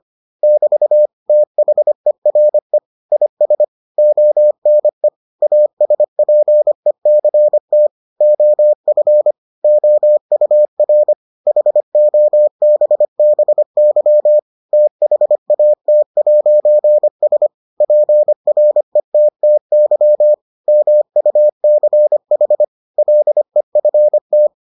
Morse code, heard as a keyed tone, about 25 wpm, 600 Hz. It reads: <BT> THERE IS ONE ASPECT OF OUR HOBBY THAT'S PRETTY MUCH LEFT